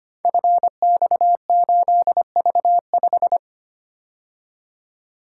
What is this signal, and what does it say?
Morse code, heard as a keyed tone, about 25 wpm, 700 Hz. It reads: FX8V5
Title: Code Groups - 5 Letter:Numbers - 25wpm